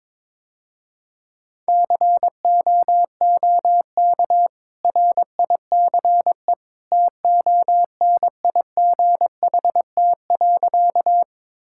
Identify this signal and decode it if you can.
Morse code, heard as a keyed tone, about 22 wpm, 690 Hz. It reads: COOK RICE TONIGHT.
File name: practice / 22wpm / Easy Sentences with Hints - 22wpm